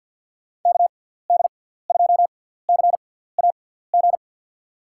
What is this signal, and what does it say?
Morse code, heard as a keyed tone, about 53 wpm, spaced as for 17 wpm, 700 Hz. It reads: KD2CAG
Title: USA Call Signs - 6 Characters - 6x Character Spacing - 50wpm